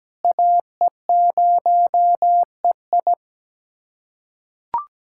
Morse code, heard as a keyed tone, about 17 wpm, 700 Hz. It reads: AE0EI
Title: USA Callsign DE Callsign - 5 Characters - 17wpm